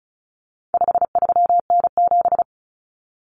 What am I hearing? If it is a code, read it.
Morse code, heard as a keyed tone, about 34 wpm, 700 Hz. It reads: 53N7
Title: Common Cut Numbers - 4-Digit 35wpm